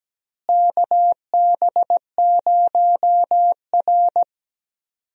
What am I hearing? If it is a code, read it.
Morse code, heard as a keyed tone, about 17 wpm, 700 Hz. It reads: KB0R
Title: USA Call Signs - 4 Characters - Repeated 3x - 17wpm